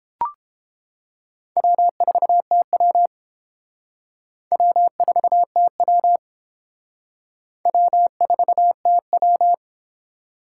Morse code, heard as a keyed tone, about 30 wpm, 700 Hz. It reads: W4TW W4TW W4TW